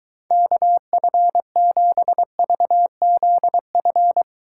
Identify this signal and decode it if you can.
Morse code, heard as a keyed tone, about 22 wpm, 700 Hz. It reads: KF7VZF